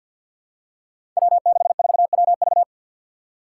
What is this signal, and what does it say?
Morse code, heard as a keyed tone, about 47 wpm, 700 Hz. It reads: W64WV
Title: Five Challenging Letter-Number Combinations 50wpm